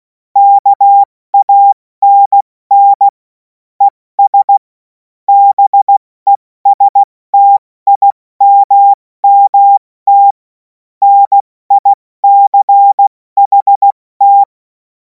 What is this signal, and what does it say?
Morse code, heard as a keyed tone, about 16 wpm, 800 Hz. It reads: KANN ES BESTIMMT NICHT
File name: Morsekurs / Witze / Witz 10. Februar 2026 70 F80